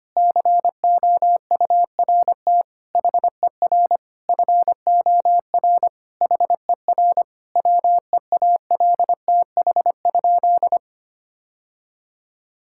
Morse code, heard as a keyed tone, about 25 wpm, 700 Hz. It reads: COURT HER FOR HER WEALTH?
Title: Sentences from Top 1600 Words 25wpm